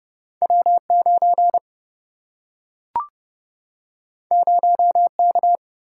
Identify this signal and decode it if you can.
Morse code, heard as a keyed tone, about 30 wpm, 700 Hz. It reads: W9 0K
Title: Single-Letter Single-Number 30wpm